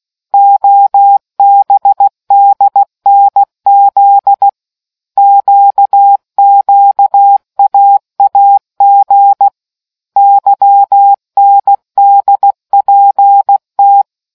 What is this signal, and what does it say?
Morse code, heard as a keyed tone, about 16 wpm, 800 Hz. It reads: OBDNZ QQAAG YNDPT